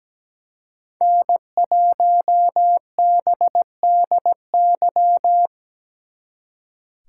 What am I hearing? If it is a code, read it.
Morse code, heard as a keyed tone, about 20 wpm, 700 Hz. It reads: N1BDY
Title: Speed Racing - USA Call Signs 17wpm